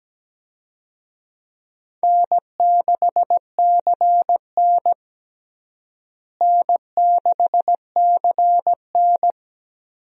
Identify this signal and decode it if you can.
Morse code, heard as a keyed tone, about 17 wpm, 700 Hz. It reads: N6CN N6CN
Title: USA Call Signs - 4 Characters - Repeated 3x - 17wpm